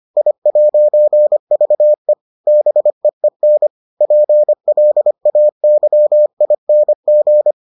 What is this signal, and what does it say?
Morse code, heard as a keyed tone, about 25 wpm, 600 Hz. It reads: I'VE BEEN PLAYING